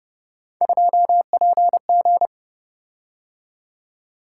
Morse code, heard as a keyed tone, about 29 wpm, 700 Hz. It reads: 2PG